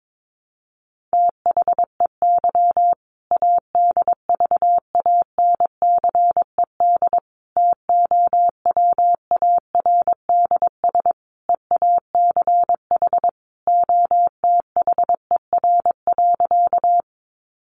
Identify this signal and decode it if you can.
Morse code, heard as a keyed tone, about 22 wpm, 700 Hz. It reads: THEY ADVANCED TOWARDS EACH OTHER.
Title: Sentences from Top 1200 Words 22wpm